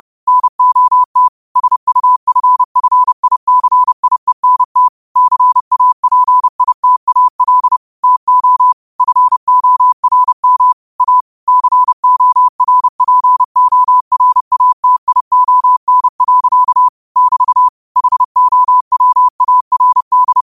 Morse code, heard as a keyed tone, about 30 wpm, 1000 Hz. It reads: NOT SUFFICIENT CAPITAL TO FORM A CORPORATION. <BT> HOWARD
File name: qotdfiles / 30wpm